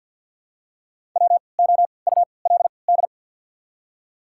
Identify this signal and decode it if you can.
Morse code, heard as a keyed tone, about 54 wpm, spaced as for 29 wpm, 700 Hz. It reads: WOULD
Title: 3x Character Spacing - Top 500 Words - 50wpm